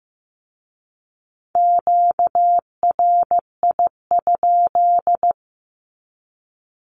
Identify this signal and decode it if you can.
Morse code, heard as a keyed tone, about 15 wpm, 700 Hz. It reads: QRI?
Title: Q-Codes with Spoken Meaning - 15wpm